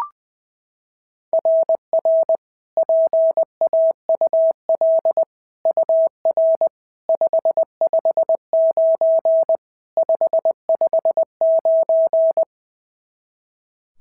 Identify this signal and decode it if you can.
Morse code, heard as a keyed tone, about 18 wpm, 660 Hz. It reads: RR PAUL UR 559 559
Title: SOTA QSO Practice 20wpm